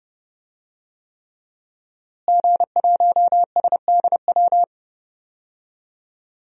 Morse code, heard as a keyed tone, about 30 wpm, 700 Hz. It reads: G1SDW
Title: DXCC-100 Call Signs - 5 Characters - Repeated 3x - 30wpm